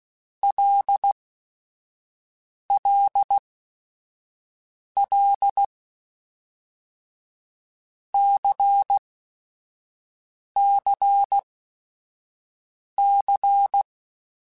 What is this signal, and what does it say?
Morse code, heard as a keyed tone, about 16 wpm, spaced as for 5 wpm, 800 Hz. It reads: LLL CCC
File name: Basic Sets / Basic Letter Groups Collective7 / LFCYQ@5